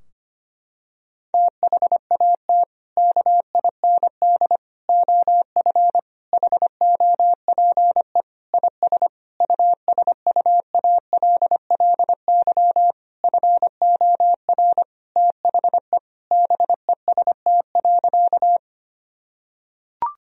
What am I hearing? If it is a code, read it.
Morse code, heard as a keyed tone, about 25 wpm, 700 Hz. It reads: THAT KIND OF HOPE IS USUALLY FOR THE BEST.
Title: Sentences from Top 1000 Words - Review Part 3 - 25wpm